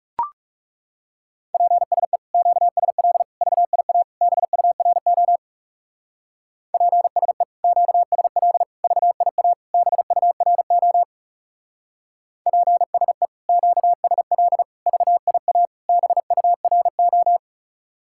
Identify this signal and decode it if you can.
Morse code, heard as a keyed tone, about 36 wpm, 700 Hz. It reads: PSE QSL VIA BURO PSE QSL VIA BURO PSE QSL VIA BURO
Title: Speed-Racing Common QSO Phrases - 30wpm